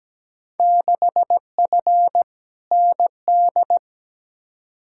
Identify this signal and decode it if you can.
Morse code, heard as a keyed tone, about 17 wpm, 700 Hz. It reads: 6F ND
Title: Field Day Exchanges Encore - 17wpm